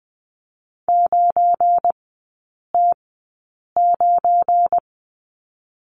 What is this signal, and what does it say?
Morse code, heard as a keyed tone, about 20 wpm, spaced as for 8 wpm, 690 Hz. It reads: 9T9
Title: 172 Lesson 28e - Number 9 - Three Characters 20wpm